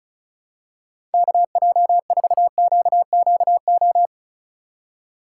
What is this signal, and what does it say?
Morse code, heard as a keyed tone, about 34 wpm, 700 Hz. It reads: KJ4QQO
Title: USA Call Signs - 6 Characters - 2x Character Spacing - 35wpm